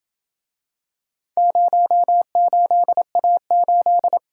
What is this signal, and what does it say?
Morse code, heard as a keyed tone, about 28 wpm, 700 Hz. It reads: 08A8